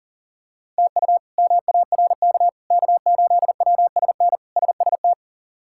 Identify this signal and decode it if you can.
Morse code, heard as a keyed tone, about 40 wpm, 700 Hz. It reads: TU MARK K8WSN SST
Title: K1USN SST QSO Practice - 40wpm